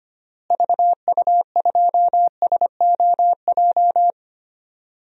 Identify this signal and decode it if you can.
Morse code, heard as a keyed tone, about 25 wpm, 700 Hz. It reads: VU2SOJ
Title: DXCC-100 Call Signs - 6 Characters - Repeated 2x - 25wpm